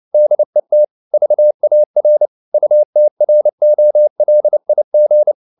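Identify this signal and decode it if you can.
Morse code, heard as a keyed tone, about 29 wpm, 600 Hz. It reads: DET VAR UTROLIG
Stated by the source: Fast (QRQ) Morse at 29 wpm, from CW med Gnister.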